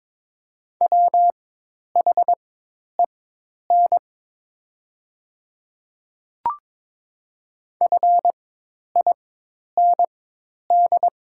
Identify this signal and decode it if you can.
Morse code, heard as a keyed tone, about 22 wpm, spaced as for 10 wpm, 700 Hz. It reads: WHEN FIND
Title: 4x Character Spacing - Top 100 Words - 22wpm